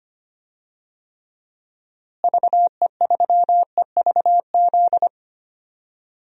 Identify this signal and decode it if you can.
Morse code, heard as a keyed tone, about 25 wpm, 700 Hz. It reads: VE3EVZ
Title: DXCC-100 Call Signs - 6 Characters - Repeated 3x - 25wpm